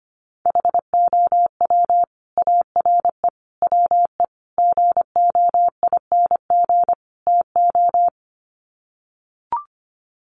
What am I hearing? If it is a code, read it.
Morse code, heard as a keyed tone, about 25 wpm, 700 Hz. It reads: HOW ARE WE GOING TO
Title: Sets of 5 Words Encore 25wpm